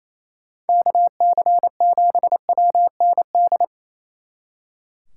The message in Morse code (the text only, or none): KC7WND